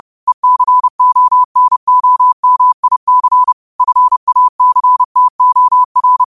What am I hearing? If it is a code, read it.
Morse code, heard as a keyed tone, about 30 wpm, 1000 Hz. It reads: ECONOMIC FACTOR
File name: qotdfiles / 30wpm